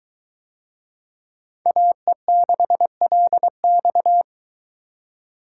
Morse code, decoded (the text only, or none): AE6LX